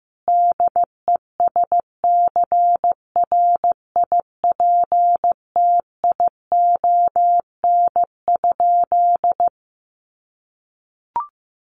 Morse code, heard as a keyed tone, about 14 wpm, 700 Hz. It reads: DESCRIPTION?
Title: Sentences from Top 1500 Words 15wpm